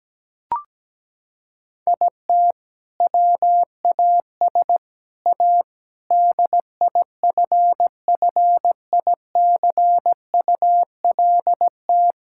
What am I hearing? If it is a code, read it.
Morse code, heard as a keyed tone, about 17 wpm, 700 Hz. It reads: IT WAS A DIFFICULT
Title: Sentences from Top 1000 Words - Review Part 6 - 17wpm